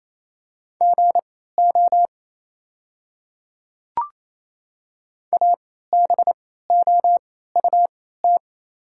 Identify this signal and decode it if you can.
Morse code, heard as a keyed tone, about 28 wpm, spaced as for 16 wpm, 690 Hz. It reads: GO ABOUT
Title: 3x Character Spacing - Top 100 Words - 28wpm